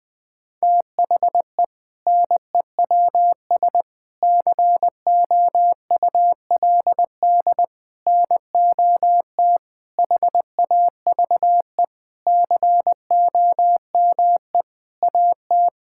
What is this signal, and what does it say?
Morse code, heard as a keyed tone, about 20 wpm, 700 Hz. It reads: THE NEWS COULD NOT HAVE COME AT
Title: Sentences with 8-Word Phrases - 20wpm